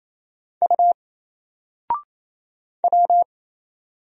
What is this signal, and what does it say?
Morse code, heard as a keyed tone, about 27 wpm, 700 Hz. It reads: U W
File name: Single-Letter-Number-28wpm